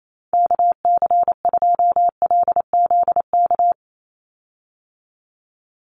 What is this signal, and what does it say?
Morse code, heard as a keyed tone, about 28 wpm, 700 Hz. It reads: KC2LZK
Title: USA Call Signs 28wpm